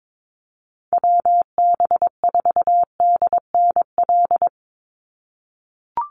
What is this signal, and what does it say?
Morse code, heard as a keyed tone, about 22 wpm, 700 Hz. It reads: WB4DNL
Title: Call Signs - 22wpm